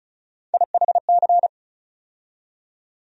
Morse code, heard as a keyed tone, about 34 wpm, 700 Hz. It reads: IHC
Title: USA Call Sign Postfixes - 3 Letters 35wpm